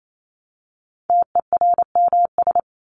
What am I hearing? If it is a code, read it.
Morse code, heard as a keyed tone, about 28 wpm, 700 Hz. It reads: TERMS
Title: ES:S Suffix - Top 100 Words with Base Words 28wpm